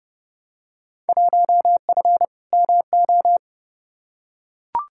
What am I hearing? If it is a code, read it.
Morse code, heard as a keyed tone, about 30 wpm, 700 Hz. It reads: 1F MO
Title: Field Day Exchanges Encore - 30wpm